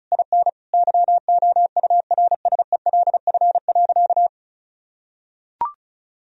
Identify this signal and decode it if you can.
Morse code, heard as a keyed tone, about 36 wpm, 700 Hz. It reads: IN YOURSELF.
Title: Sentences from Top 1000 Words - Review Part 1 - 35wpm